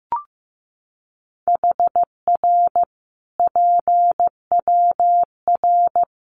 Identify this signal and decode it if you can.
Morse code, heard as a keyed tone, about 15 wpm, 700 Hz. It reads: HR PWR
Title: Common QSO Phrases 15wpm